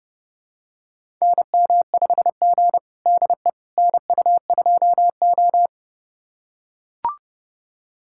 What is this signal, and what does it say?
Morse code, heard as a keyed tone, about 32 wpm, 700 Hz. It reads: NM5G DE NU2O
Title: USA Callsign DE Callsign - 4 Characters - 30wpm